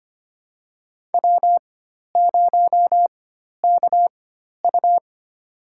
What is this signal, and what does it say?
Morse code, heard as a keyed tone, about 25 wpm, spaced as for 12 wpm, 700 Hz. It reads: W0KU
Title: USA Call Signs - 4 Characters - 4x Character Spacing - 25wpm